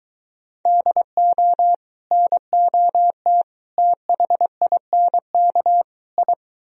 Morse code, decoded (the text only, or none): DO NOT THINK I